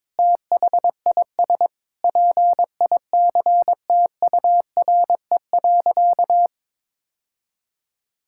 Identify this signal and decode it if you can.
Morse code, heard as a keyed tone, about 22 wpm, 700 Hz. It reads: THIS PICTURE.
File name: practice / 22wpm / Sentences with 4-Word Phrases - 22wpm